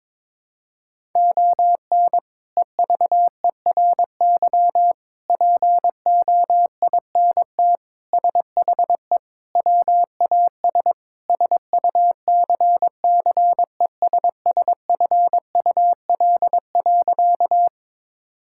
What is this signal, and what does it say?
Morse code, heard as a keyed tone, about 21 wpm, 700 Hz. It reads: ON EVERY POINT SHE WAS SUCCESSFUL.
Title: Sentences from Top 1000 Words - Review Part 6 - 22wpm